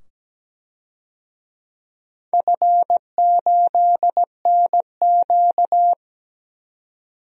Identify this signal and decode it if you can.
Morse code, heard as a keyed tone, about 17 wpm, 700 Hz. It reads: F8NQ
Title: DXCC-100 Call Signs - 4 Characters - 17wpm